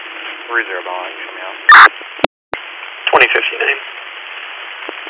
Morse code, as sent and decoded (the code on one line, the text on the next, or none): none